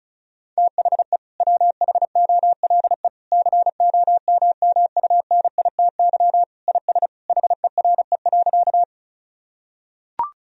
Morse code, heard as a keyed tone, about 34 wpm, 700 Hz. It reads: THE WHOLE COMMUNITY IS HERE.
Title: Sentences from Top 1000 Words - Review Part 4 - 35wpm